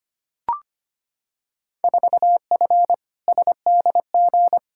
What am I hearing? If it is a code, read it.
Morse code, heard as a keyed tone, about 25 wpm, 700 Hz. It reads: 4F SDG
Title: Field Day Exchanges - 25wpm